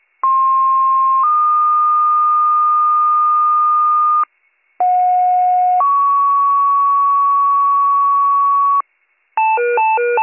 Two-tone page, heard as a one-second tone, then a three-second tone, about 1060 then 1220 Hz, about 730 then 1120 Hz. It is heard from 0.2 to 8.8 s.